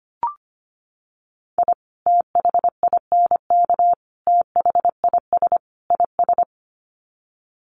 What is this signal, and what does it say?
Morse code, heard as a keyed tone, about 25 wpm, 700 Hz. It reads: I THINK THIS IS
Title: Sets of 4 Words 25wpm